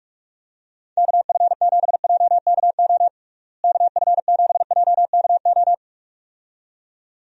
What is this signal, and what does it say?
Morse code, heard as a keyed tone, about 42 wpm, 700 Hz. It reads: KF7JKO KF7JKO
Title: USA Call Signs - 6 Characters - Repeated 2x - 45wpm